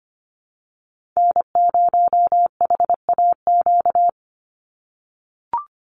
Morse code, heard as a keyed tone, about 25 wpm, 700 Hz. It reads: N0HAQ
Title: USA Call Signs 25wpm